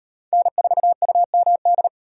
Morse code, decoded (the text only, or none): N4UMD